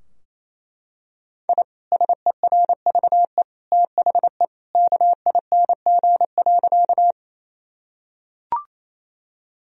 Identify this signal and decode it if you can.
Morse code, heard as a keyed tone, about 28 wpm, 700 Hz. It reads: I SERVE THE KING.
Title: Sentences from Top 1000 Words - Review Part 2 - 28wpm